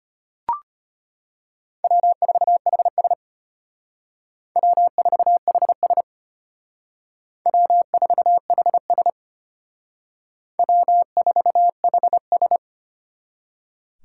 Morse code, heard as a keyed tone, about 32 wpm, 700 Hz. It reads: W4HS W4HS W4HS W4HS